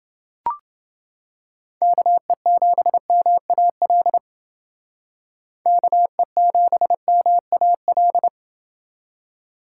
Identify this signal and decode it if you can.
Morse code, heard as a keyed tone, about 29 wpm, 700 Hz. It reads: KE7MAL KE7MAL